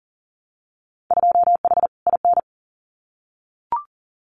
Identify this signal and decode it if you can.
Morse code, heard as a keyed tone, about 38 wpm, 700 Hz. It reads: 2H IN
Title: Winter Field Day Exchanges Encore 40wpm